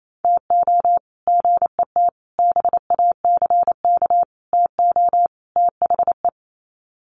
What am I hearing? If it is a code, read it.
Morse code, heard as a keyed tone, about 28 wpm, 700 Hz. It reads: TO GET BACK TO THE
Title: Sets of 5 Words Encore 28wpm